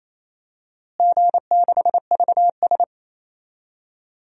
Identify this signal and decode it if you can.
Morse code, heard as a keyed tone, about 28 wpm, 700 Hz. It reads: G6VS
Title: DXCC-100 Call Signs - 4 Characters - Repeated 2x - 28wpm